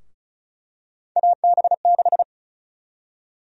Morse code, heard as a keyed tone, about 34 wpm, 700 Hz. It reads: AB6
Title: USA Call Sign Prefixes - Repeated 2x - 35wpm